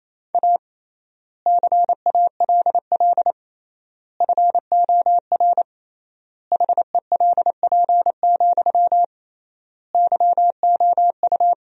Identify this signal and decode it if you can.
Morse code, heard as a keyed tone, about 27 wpm, 700 Hz. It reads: A CALL FOR HELP, YOU
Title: 3x Word-Spacing - Sentences from Top 300 Words - 28wpm